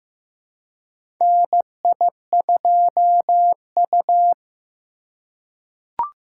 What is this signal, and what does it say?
Morse code, heard as a keyed tone, about 15 wpm, 700 Hz. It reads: NI2U
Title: USA Call Signs - 4 Characters - 2x Character Spacing - 15wpm